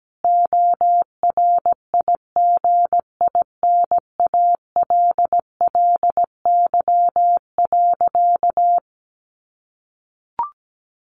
Morse code, heard as a keyed tone, about 17 wpm, 700 Hz. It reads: ORIGINALLY.
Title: A Sound Of Thunder 17wpm